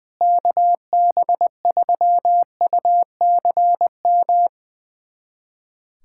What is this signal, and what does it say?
Morse code, heard as a keyed tone, about 21 wpm, 700 Hz. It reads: KB3UCM
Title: USA Callsign DE Callsign 20wpm